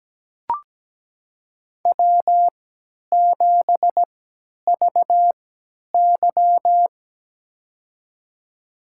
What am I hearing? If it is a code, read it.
Morse code, heard as a keyed tone, about 17 wpm, spaced as for 10 wpm, 700 Hz. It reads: W7VY
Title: USA Call Signs - 4 Characters - 3x Character Spacing - 17wpm